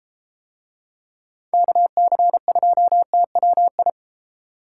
Morse code, heard as a keyed tone, about 32 wpm, 690 Hz. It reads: KC2TWI